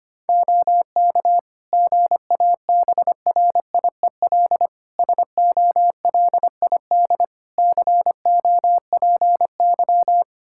Morse code, heard as a keyed tone, about 25 wpm, 700 Hz. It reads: OK GABRIEL SOLID COPY
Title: Practice QSO - Exchange 3 - More Info - 25wpm